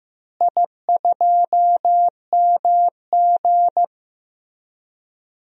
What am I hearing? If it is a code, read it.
Morse code, heard as a keyed tone, about 16 wpm, 700 Hz. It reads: I2MG
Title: DXCC-100 Call Signs - 4 Characters - Repeated 2x - 15wpm